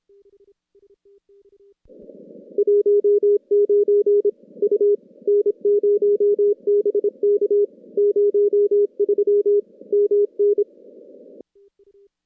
Morse code, heard as a keyed tone, about 27 wpm, 410 Hz. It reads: 19 U N0BK 03 MN
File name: contests / CW-SS-k5tr- -2100